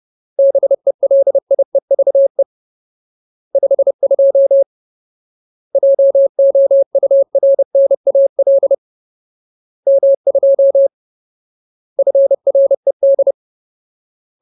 Morse code, heard as a keyed tone, about 30 wpm, 550 Hz. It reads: BELIEVE 52 JOURNAL M2 FRED
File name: Ultimate Word List / 30WPM Ultimate